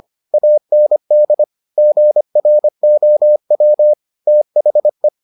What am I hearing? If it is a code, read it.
Morse code, heard as a keyed tone, about 25 wpm, 600 Hz. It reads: AND GROW THE